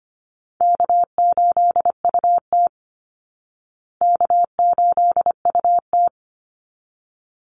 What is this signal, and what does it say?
Morse code, heard as a keyed tone, about 25 wpm, 700 Hz. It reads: K8UT K8UT
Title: Call Signs Repeated 3x - 25wpm